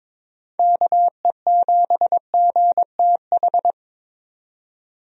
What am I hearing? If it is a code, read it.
Morse code, heard as a keyed tone, about 22 wpm, 700 Hz. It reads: KE7GTH
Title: USA Call Signs - 6 Characters - 22wpm